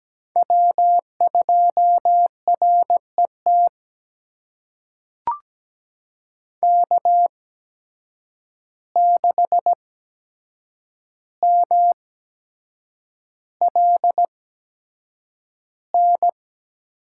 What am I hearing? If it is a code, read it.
Morse code, heard as a keyed tone, about 17 wpm, spaced as for 7 wpm, 700 Hz. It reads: W2RET K6MLN
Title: USA Call Signs - 8x Character Spacing - 17wpm